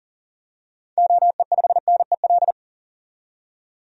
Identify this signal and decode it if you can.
Morse code, heard as a keyed tone, about 39 wpm, 700 Hz. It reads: OE5NEL
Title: DXCC-100 Call Signs - 6 Characters - 40wpm